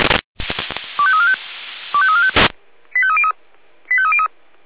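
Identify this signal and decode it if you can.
Five-tone selective call (ZVEI), heard as quick runs of five tones, tones of about 70 ms. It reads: 25345 25345 85303 85303
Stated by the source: Received by SDR; ZVEI five-tone call 25212, channel 496.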